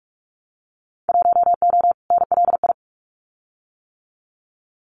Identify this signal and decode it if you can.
Morse code, heard as a keyed tone, about 46 wpm, 700 Hz. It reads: 1O NLI
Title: Winter Field Day Exchanges 45wpm